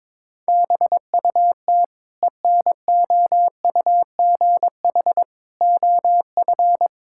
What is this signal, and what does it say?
Morse code, heard as a keyed tone, about 21 wpm, 700 Hz. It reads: BUT ENOUGH OF
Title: Sentences from Top 1000 Words - Review Part 6 - 22wpm